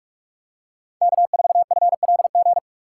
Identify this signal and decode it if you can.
Morse code, heard as a keyed tone, about 44 wpm, 700 Hz. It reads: K4FLG